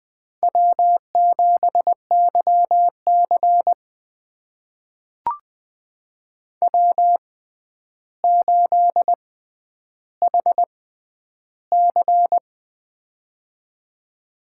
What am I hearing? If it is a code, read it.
Morse code, heard as a keyed tone, about 20 wpm, spaced as for 10 wpm, 700 Hz. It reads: W7YC W8HC
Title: USA Call Signs - 4 Characters - 6x Character Spacing - 20wpm